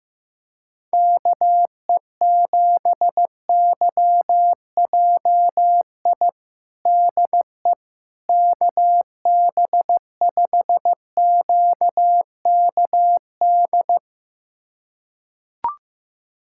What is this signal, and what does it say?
Morse code, heard as a keyed tone, about 15 wpm, 700 Hz. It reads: KE7YJI DE KB5QKD
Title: USA Callsign DE Callsign - 6 Characters - 15wpm